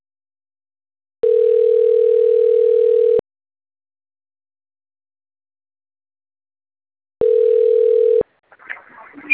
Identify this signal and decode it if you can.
Ringback tone (North America), 440 + 480 Hz.